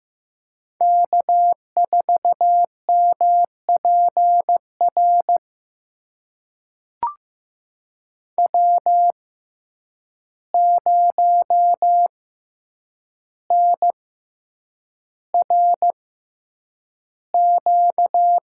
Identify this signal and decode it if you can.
Morse code, heard as a keyed tone, about 15 wpm, spaced as for 8 wpm, 700 Hz. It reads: K4MPR W0NRQ
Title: USA Call Signs - 5 Characters - 6x Character Spacing - 15wpm